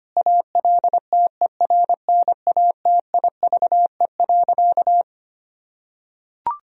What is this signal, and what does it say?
Morse code, heard as a keyed tone, about 25 wpm, 700 Hz. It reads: ALTERNATIVE.
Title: Sentences from Top 1000 Words - Review Part 4 - 25wpm